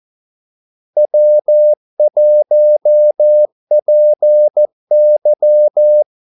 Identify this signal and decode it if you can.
Morse code, heard as a keyed tone, about 14 wpm, 600 Hz. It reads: W1PY
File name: Callsigns-14wpm